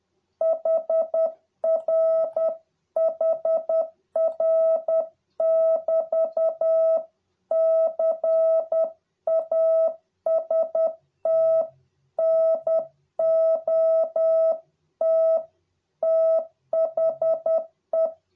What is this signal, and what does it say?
Morse code, heard as a keyed tone, about 10 wpm, 650 Hz. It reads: HR HR <BT> CAST NOT THE